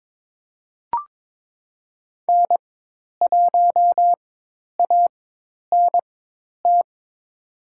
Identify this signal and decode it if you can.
Morse code, heard as a keyed tone, about 22 wpm, spaced as for 10 wpm, 700 Hz. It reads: N1ANT
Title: USA Call Signs - 5 Characters - 4x Character Spacing - 22wpm